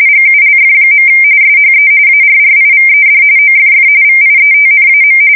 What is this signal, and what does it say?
RTTY, about 45.45 baud. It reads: GULF. SHE ARRIVED OFF VIETNAM
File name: fsk170 / USS-New-Jersey